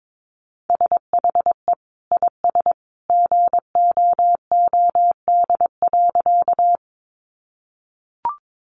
Morse code, heard as a keyed tone, about 22 wpm, 700 Hz. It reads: SHE IS GOOD.
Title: Easy Phrases 22wpm